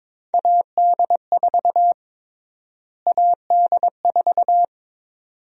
Morse code, heard as a keyed tone, about 22 wpm, 700 Hz. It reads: AD4 AD4
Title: USA Call Sign Prefixes - Repeated 2x - 22wpm